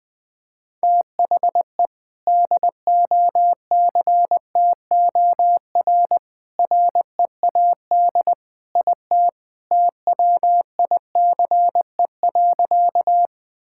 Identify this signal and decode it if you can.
Morse code, heard as a keyed tone, about 20 wpm, 700 Hz. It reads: THE DOCTOR READ IT TWICE.
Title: Sentences from Top 1600 Words 20wpm